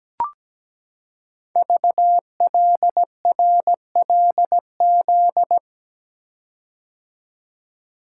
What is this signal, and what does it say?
Morse code, heard as a keyed tone, about 17 wpm, 700 Hz. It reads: VLRLZ
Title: Code Groups - 5 Letters - 17wpm